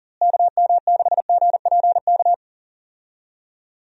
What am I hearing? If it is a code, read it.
Morse code, heard as a keyed tone, about 39 wpm, 700 Hz. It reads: KM6GPK
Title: USA Call Signs - 6 Characters - Repeated 2x - 40wpm